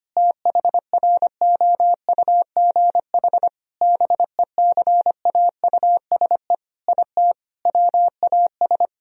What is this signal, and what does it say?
Morse code, heard as a keyed tone, about 25 wpm, 700 Hz. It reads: THROUGH BECAUSE IT WAS
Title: Sentences from Top 1000 Words - Review Part 1 - 25wpm